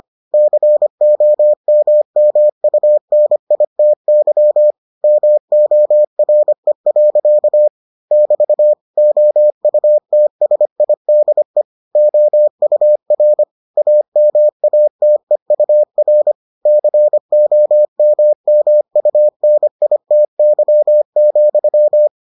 Morse code, heard as a keyed tone, about 25 wpm, 600 Hz. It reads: COMMUNITY MORE. <BT> OUTSIDE OUR AMATEUR COMMUNITY,